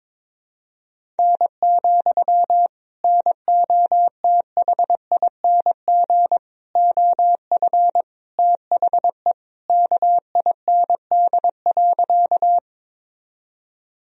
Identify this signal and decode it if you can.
Morse code, heard as a keyed tone, about 21 wpm, 700 Hz. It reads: N, NOTHING OF THE KIND.
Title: Sentences from Top 500 Words and Abbreviations 22wpm